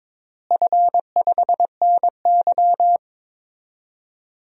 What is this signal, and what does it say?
Morse code, heard as a keyed tone, about 21 wpm, 700 Hz. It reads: F5NY
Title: DXCC-100 Call Signs - 4 Characters - Repeated 2x - 22wpm